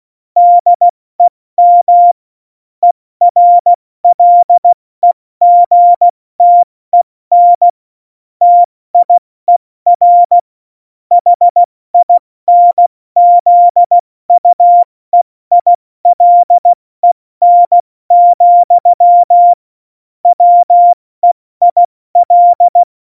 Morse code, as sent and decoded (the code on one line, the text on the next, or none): -.. -- / . .-. .-.. --. - . -. / - .. .-. / .... .. -. --.. ..- . .. .-.. -. --..-- / .-- . .. .-..
DEM ERLEGTEN TIER HINZUEILEN, WEIL